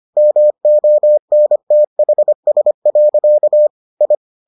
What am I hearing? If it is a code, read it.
Morse code, heard as a keyed tone, about 25 wpm, 600 Hz. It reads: MONTHS. I